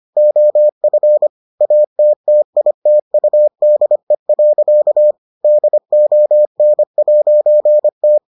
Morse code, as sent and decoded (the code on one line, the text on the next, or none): --- ..-. / .- - - .. - ..- -.. .-.-.- / -.. --- -. .----. -
OF ATTITUDE. DON'T